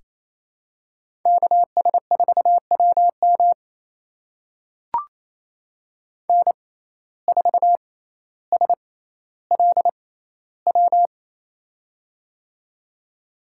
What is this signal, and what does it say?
Morse code, heard as a keyed tone, about 28 wpm, spaced as for 14 wpm, 700 Hz. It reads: KS4WM N4SLW
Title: USA Call Signs - 6x Character Spacing - 28wpm